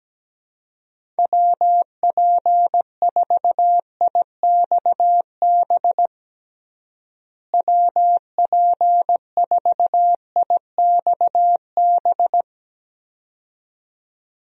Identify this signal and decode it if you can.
Morse code, heard as a keyed tone, about 17 wpm, 700 Hz. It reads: WP4IXB WP4IXB
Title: USA Call Signs - 6 Characters - Repeated 2x - 17wpm